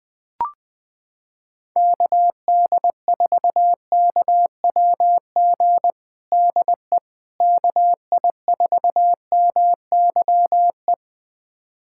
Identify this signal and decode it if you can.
Morse code, heard as a keyed tone, about 20 wpm, 700 Hz. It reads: KD4KWG DE KI4MYE
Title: USA Callsign DE Callsign 20wpm